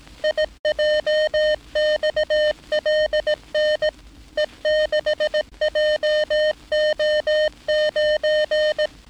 Morse code, heard as a keyed tone, about 18 wpm, 610 Hz. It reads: IJXLN E6JO9
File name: training / NAVPERS 11049to11060 / NAVPERS 11059 RA RB